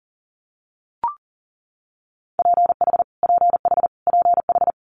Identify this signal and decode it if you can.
Morse code, heard as a keyed tone, about 42 wpm, 700 Hz. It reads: PH PH PH
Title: Multi-letter Phonemes Repeated 3x 40wpm